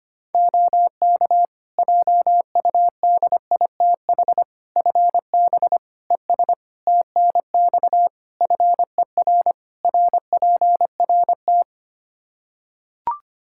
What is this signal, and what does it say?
Morse code, heard as a keyed tone, about 25 wpm, 700 Hz. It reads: OK JUDITH FB ES TNX FER RPRT
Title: Practice QSO - Exchange 2 - Info - 25wpm